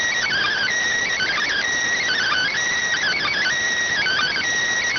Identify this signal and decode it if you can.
Emergency Alert System (SAME) tones, heard as harsh buzzing bursts. It is heard from 0.2 to 4.4 s.